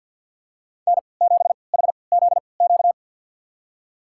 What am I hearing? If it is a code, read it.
Morse code, heard as a keyed tone, about 52 wpm, spaced as for 29 wpm, 700 Hz. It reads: N7HZQ
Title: USA Call Signs - 5 Characters - 3x Character Spacing - 50wpm